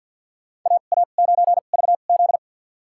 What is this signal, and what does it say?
Morse code, heard as a keyed tone, about 52 wpm, spaced as for 36 wpm, 700 Hz. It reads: AA9VZ
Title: USA Call Signs - 5 Characters - 2x Character Spacing - 50wpm